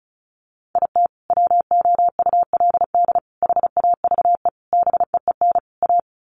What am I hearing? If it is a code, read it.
Morse code, heard as a keyed tone, about 34 wpm, 700 Hz. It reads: IT WOULD HAVE BEEN A
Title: Sets of 5 Words Encore 35wpm